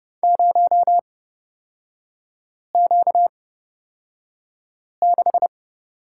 Morse code, heard as a keyed tone, about 29 wpm, spaced as for 5 wpm, 700 Hz. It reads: 0Q6
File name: Lesson 38e - Letter q - Three Characters 30wpm